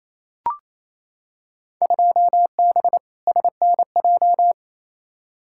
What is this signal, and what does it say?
Morse code, heard as a keyed tone, about 28 wpm, 700 Hz. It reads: 2B SNJ
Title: Field Day Exchanges Encore - 28wpm